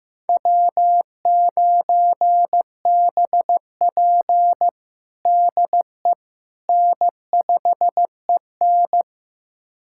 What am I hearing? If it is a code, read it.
Morse code, heard as a keyed tone, about 15 wpm, 700 Hz. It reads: W9BP DE N5EN
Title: USA Callsign DE Callsign - 4 Characters - 15wpm